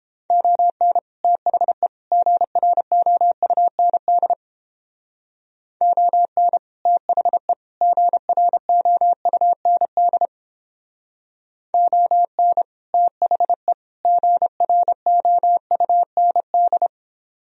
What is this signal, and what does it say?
Morse code, heard as a keyed tone, about 31 wpm, 700 Hz. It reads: ON THE GROUND ON THE GROUND ON THE GROUND